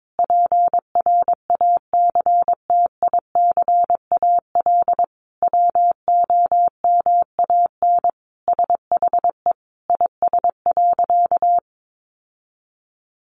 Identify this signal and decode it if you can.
Morse code, heard as a keyed tone, about 22 wpm, 700 Hz. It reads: PRACTICAL WOMAN SHE IS.
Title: Sentences from Top 1400 Words 22wpm